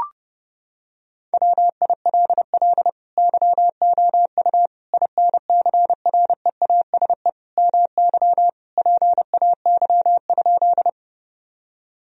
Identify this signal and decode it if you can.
Morse code, heard as a keyed tone, about 30 wpm, 700 Hz. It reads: WILL YOU INCREASE MY PAY?
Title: Sentences from Top 1000 Words - Review Part 4 - 30wpm